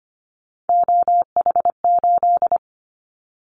Morse code, heard as a keyed tone, about 25 wpm, 700 Hz. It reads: OH8
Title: Callsign Prefixes for Top 25 DXCC - Repeated 3x - 25wpm